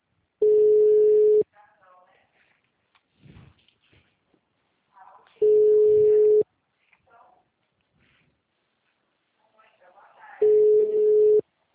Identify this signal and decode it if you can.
Ringback tone (Europe, 425 Hz), with speech.
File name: out- -2009- - - .31971